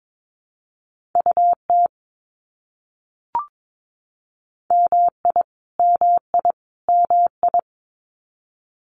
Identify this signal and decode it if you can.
Morse code, heard as a keyed tone, about 22 wpm, 700 Hz. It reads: UT MI MI MI
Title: US States Repeated 3x 22wpm